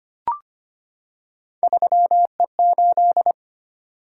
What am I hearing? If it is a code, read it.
Morse code, heard as a keyed tone, about 25 wpm, 700 Hz. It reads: 3E8
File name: Three Letter-Number Combinations - 25wpm